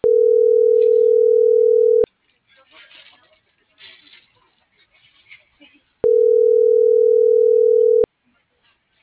Ringback tone (North America), 440 + 480 Hz.